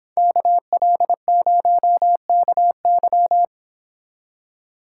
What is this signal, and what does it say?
Morse code, heard as a keyed tone, about 26 wpm, 700 Hz. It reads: KL0KY